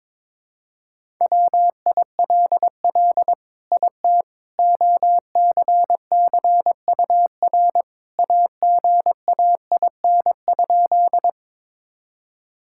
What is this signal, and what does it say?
Morse code, heard as a keyed tone, about 22 wpm, 700 Hz. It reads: WILL IT OCCUR AGAIN?
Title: Sentences from Top 1000 Words - Review Part 6 - 22wpm